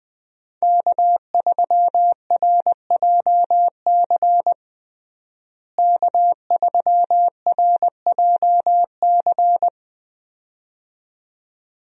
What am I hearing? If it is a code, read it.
Morse code, heard as a keyed tone, about 20 wpm, 700 Hz. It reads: K3RJC K3RJC
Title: USA Call Signs - 5 Characters - Repeated 2x - 20wpm